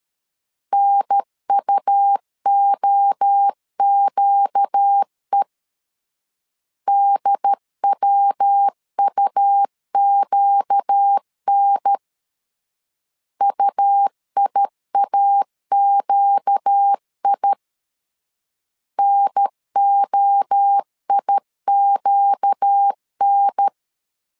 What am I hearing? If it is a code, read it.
Morse code, heard as a keyed tone, about 13 wpm, 800 Hz. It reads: NUOQE DWUQN UIAQI NOIQN
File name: LEZIONI / 4 Q 12wpm 1833